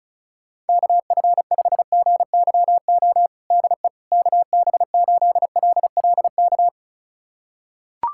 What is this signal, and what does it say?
Morse code, heard as a keyed tone, about 36 wpm, 700 Hz. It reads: KF5GYO DE KB8LLK
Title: USA Callsign DE Callsign 35wpm